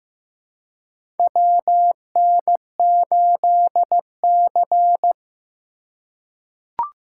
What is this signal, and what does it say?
Morse code, heard as a keyed tone, about 15 wpm, 700 Hz. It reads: WN8C
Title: USA Call Signs - 4 Characters - 2x Character Spacing - 15wpm